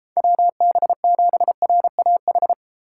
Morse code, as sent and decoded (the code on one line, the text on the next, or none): .-- -... --... .-. .- ....
WB7RAH